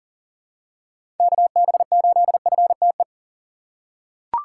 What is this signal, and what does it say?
Morse code, heard as a keyed tone, about 40 wpm, 700 Hz. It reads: KB8FTE
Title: USA Call Signs - 3x Character Spacing - 40wpm